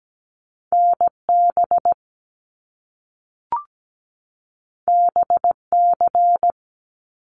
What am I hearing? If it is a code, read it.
Morse code, heard as a keyed tone, about 17 wpm, 700 Hz. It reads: NB BC
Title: Canadian Provinces and Territories 17wpm